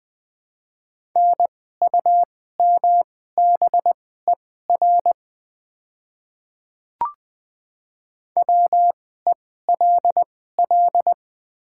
Morse code, heard as a keyed tone, about 20 wpm, spaced as for 14 wpm, 700 Hz. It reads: NUMBER WELL
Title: 2x Character Spacing - Top 500 Words - 20wpm